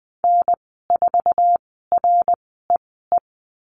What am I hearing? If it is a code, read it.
Morse code, heard as a keyed tone, about 19 wpm, spaced as for 14 wpm, 700 Hz. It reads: N4REE
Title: 076 Lesson 13g - Letter L - Call Signs 20wpm